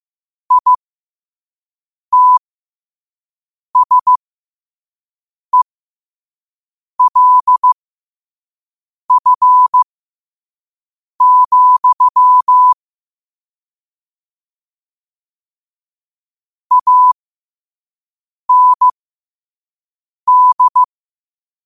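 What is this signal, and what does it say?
Morse code, heard as a keyed tone, about 15 wpm, spaced as for 5 wpm, 1000 Hz. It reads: ITSELF, AND
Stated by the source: Morse code practice at 5 wpm.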